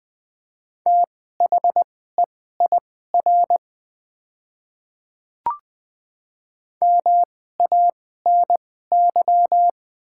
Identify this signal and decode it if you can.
Morse code, heard as a keyed tone, about 20 wpm, spaced as for 14 wpm, 700 Hz. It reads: THEIR MANY
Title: 2x Character Spacing - Top 100 Words - 20wpm